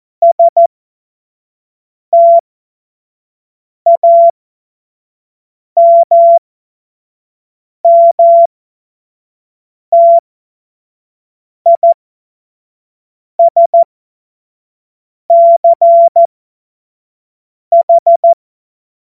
Morse code, heard as a keyed tone, about 14 wpm, spaced as for 5 wpm, 680 Hz. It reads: STAMMTISCH
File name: Morsekurs / Witze / Witz 22. Februar 2022 25 F70